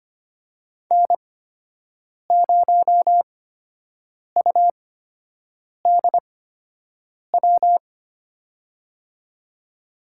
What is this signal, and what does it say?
Morse code, heard as a keyed tone, about 25 wpm, spaced as for 7 wpm, 700 Hz. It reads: N0UDW
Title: USA Call Signs - 5 Characters - 8x Character Spacing - 25wpm